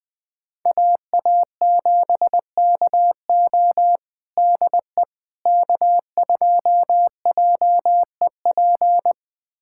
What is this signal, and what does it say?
Morse code, heard as a keyed tone, about 21 wpm, 700 Hz. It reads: AA7KO DE K2JEP
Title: USA Callsign DE Callsign - 5 Characters - 20wpm